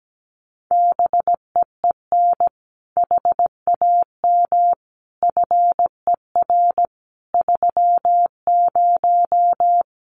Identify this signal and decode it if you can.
Morse code, heard as a keyed tone, about 17 wpm, 700 Hz. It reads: BEEN HAM FER 30
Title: Common QSO Phrases Encore 17wpm